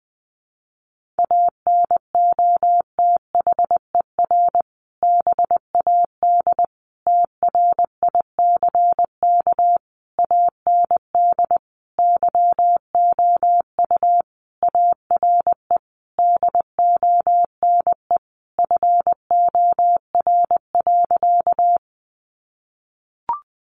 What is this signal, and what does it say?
Morse code, heard as a keyed tone, about 20 wpm, 700 Hz. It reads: ANOTHER BAD TRICK AND YOU ARE DONE FOR.
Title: Sentences from Top 1500 Words 20wpm